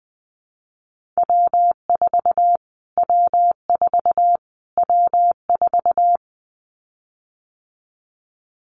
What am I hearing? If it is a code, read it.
Morse code, heard as a keyed tone, about 20 wpm, 700 Hz. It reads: W4 W4 W4
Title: Callsign Prefixes for Top 25 DXCC - Repeated 3x - 20wpm